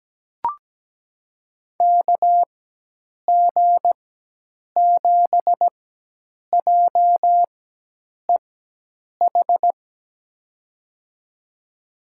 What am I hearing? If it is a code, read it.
Morse code, heard as a keyed tone, about 17 wpm, spaced as for 8 wpm, 700 Hz. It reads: KG7JEH
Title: USA Call Signs - 6 Characters - 4x Character Spacing - 17wpm